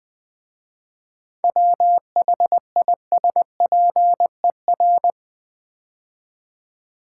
Morse code, heard as a keyed tone, about 20 wpm, 700 Hz. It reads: WHISPER
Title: 7 Letter Words - First 3 Letters Given - 20wpm